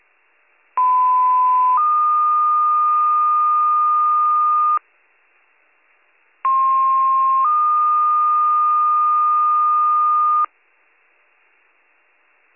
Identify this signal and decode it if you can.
Two-tone page, heard as a one-second tone, then a three-second tone, about 1010 then 1220 Hz, about 1060 then 1220 Hz. It is heard from 0.8 to 10.5 s.